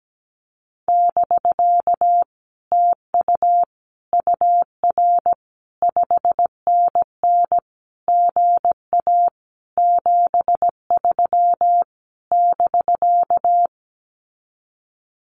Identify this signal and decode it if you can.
Morse code, heard as a keyed tone, about 17 wpm, 700 Hz. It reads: <BK> TU UR 5NN GA 73 <BK>
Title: Common QSO Phrases Encore 17wpm